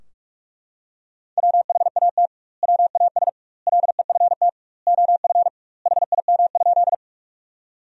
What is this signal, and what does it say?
Morse code, heard as a keyed tone, about 43 wpm, 700 Hz. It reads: WHAT WAS LEFT OF HIM?
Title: 2x Word-Spacing - Sentences from Top 300 Words - 45wpm